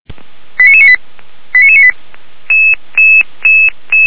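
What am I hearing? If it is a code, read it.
Five-tone selective call (ZVEI), heard as quick runs of five tones, tones of about 70 ms. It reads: 89998 89998
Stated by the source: Five-tone ZVEI call 89996 received by SDR.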